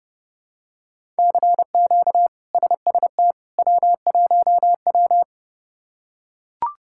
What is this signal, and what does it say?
Morse code, heard as a keyed tone, about 29 wpm, 700 Hz. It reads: CQ SST W1W
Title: K1USN SST QSO Practice - 30wpm